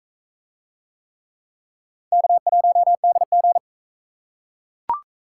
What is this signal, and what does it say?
Morse code, heard as a keyed tone, about 42 wpm, 700 Hz. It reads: K1DG